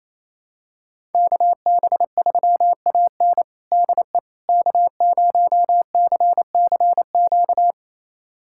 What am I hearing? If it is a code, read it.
Morse code, heard as a keyed tone, about 27 wpm, 700 Hz. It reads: KB3AN DE K0CCQ
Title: USA Callsign DE Callsign - 5 Characters - 28wpm